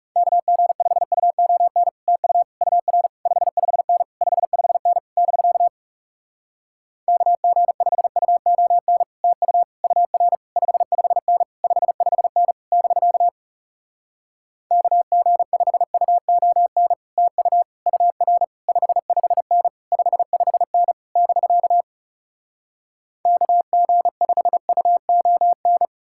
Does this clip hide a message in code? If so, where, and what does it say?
Yes: Morse code from 0.2 to 25.9 s, heard as a keyed tone, about 40 wpm, 700 Hz. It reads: KG5UON TU UR 55N 55N <BK> KG5UON TU UR 55N 55N <BK> KG5UON TU UR 55N 55N <BK> KG5UON